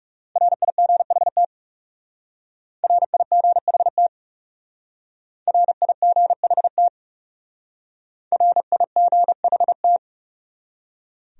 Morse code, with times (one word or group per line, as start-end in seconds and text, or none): 0.4-1.4 RIGHT
2.8-4.1 RIGHT
5.5-6.9 RIGHT
8.3-10.0 RIGHT